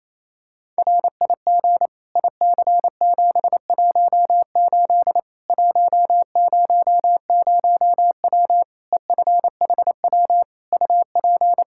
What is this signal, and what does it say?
Morse code, heard as a keyed tone, about 27 wpm, 700 Hz. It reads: RIG IC718 100W EFHW UP
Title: Practice QSO - Exchange 2 - Info - 28wpm